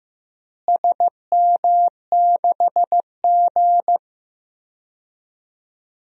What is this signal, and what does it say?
Morse code, heard as a keyed tone, about 15 wpm, 700 Hz. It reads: SM6G
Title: DXCC-100 Call Signs - 4 Characters - Repeated 2x - 15wpm